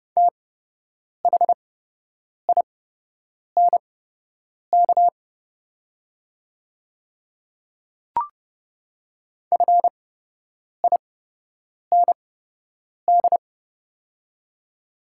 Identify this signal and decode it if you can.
Morse code, heard as a keyed tone, about 30 wpm, spaced as for 8 wpm, 700 Hz. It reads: THINK FIND
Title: 8x Character Spacing - Top 200 Words - 30wpm